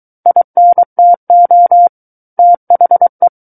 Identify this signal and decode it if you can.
Morse code, heard as a keyed tone, about 23 wpm, 700 Hz. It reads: INTO THE
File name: Book One / 14-Chapter 14 - 23 WPM